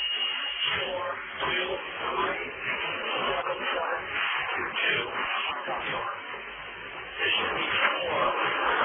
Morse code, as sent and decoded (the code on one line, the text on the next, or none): none